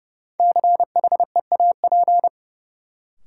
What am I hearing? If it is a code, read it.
Morse code, heard as a keyed tone, about 30 wpm, 700 Hz. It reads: CHEAP
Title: Sentences from Top 1200 Words - New Word Drill Alt - 30wpm